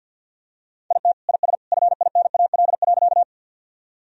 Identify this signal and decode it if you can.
Morse code, heard as a keyed tone, about 52 wpm, 700 Hz. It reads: IT IS FINAL.
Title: Sentences from Top 1000 Words - Review Part 2 - 50wpm